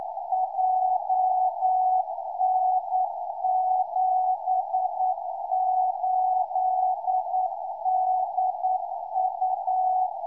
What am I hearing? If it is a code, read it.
Morse code, heard as a keyed tone, about 9 wpm, 740 Hz, with radio noise. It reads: JN78DU